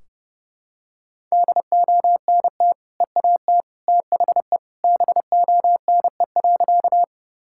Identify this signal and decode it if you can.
Morse code, heard as a keyed tone, about 30 wpm, 700 Hz. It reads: DONT EAT THE BONE.
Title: Sentences from Top 1600 Words 30wpm